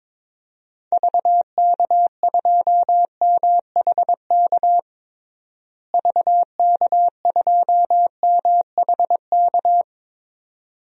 Morse code, heard as a keyed tone, about 23 wpm, 700 Hz. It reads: VK2MHK VK2MHK
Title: DXCC-100 Call Signs - 6 Characters - Repeated 2x - 22wpm